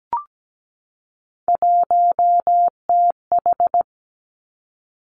Morse code, heard as a keyed tone, about 17 wpm, 700 Hz. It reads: 1TH
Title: Three Challenging Letter-Number Combinations 17wpm